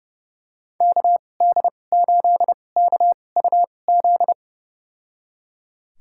Morse code, heard as a keyed tone, about 30 wpm, spaced as for 22 wpm, 700 Hz. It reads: KD8KUZ
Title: USA Call Signs - 6 Characters - 2x Character Spacing - 30wpm